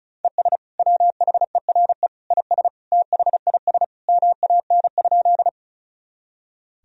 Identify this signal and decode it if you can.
Morse code, heard as a keyed tone, about 36 wpm, 700 Hz. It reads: ES WHERE IS THIS MAN?
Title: Sentences from Top 200 Words and Abbreviations 35wpm